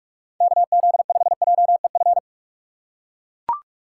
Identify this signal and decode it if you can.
Morse code, heard as a keyed tone, about 42 wpm, 700 Hz. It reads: KZ5JEF
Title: USA Call Signs - 6x Character Spacing - 45wpm